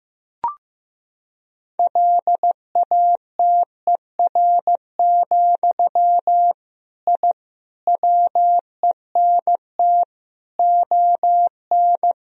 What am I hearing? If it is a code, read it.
Morse code, heard as a keyed tone, about 15 wpm, 700 Hz. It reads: LATER, I WENT ON
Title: Sentences from Top 1000 Words - Review Part 4 - 15wpm